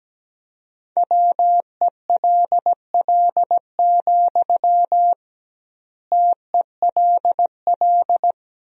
Morse code, heard as a keyed tone, about 17 wpm, 700 Hz. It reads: WELL, TELL
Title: 2x Word-Spacing - Sentences from Top 200 Words - 17wpm